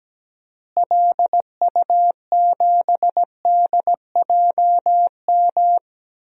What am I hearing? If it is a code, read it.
Morse code, heard as a keyed tone, about 17 wpm, 700 Hz. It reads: LU7DJM
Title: DXCC-100 Call Signs - 6 Characters - Repeated 2x - 17wpm